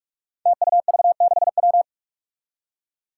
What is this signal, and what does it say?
Morse code, heard as a keyed tone, about 44 wpm, 700 Hz. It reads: TUV6W
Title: Five Challenging Letter-Number Combinations 45wpm